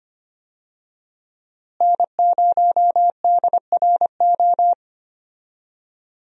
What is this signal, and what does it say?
Morse code, heard as a keyed tone, about 25 wpm, 700 Hz. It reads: N0DRO
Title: USA Call Signs - 5 Characters - Repeated 3x - 25wpm